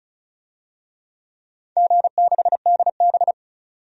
Morse code, heard as a keyed tone, about 36 wpm, 700 Hz. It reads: G6DB
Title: DXCC-100 Call Signs - 4 Characters - 35wpm